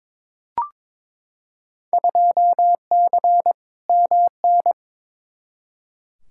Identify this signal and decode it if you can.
Morse code, heard as a keyed tone, about 22 wpm, 700 Hz. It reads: 2C MN
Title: Field Day Exchanges Encore - 22wpm